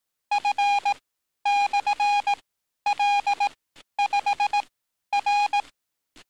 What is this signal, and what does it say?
Morse code, heard as a keyed tone, about 18 wpm, spaced as for 11 wpm, 820 Hz. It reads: F/L5R
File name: training / NAVPERS 11049to11060 / NAVPERS 11050 RA RB